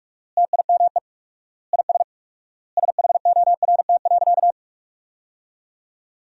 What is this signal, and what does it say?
Morse code, heard as a keyed tone, about 46 wpm, 700 Hz. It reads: TIME IS SHORT.
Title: 4x Word-Spacing - Sentences from Top 500 Words - 45wpm